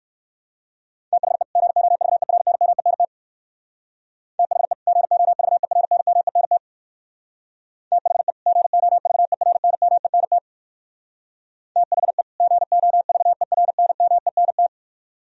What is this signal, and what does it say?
Morse code, heard as a keyed tone, about 58 wpm, 700 Hz. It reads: THE GOVERNMENT THE GOVERNMENT THE GOVERNMENT THE GOVERNMENT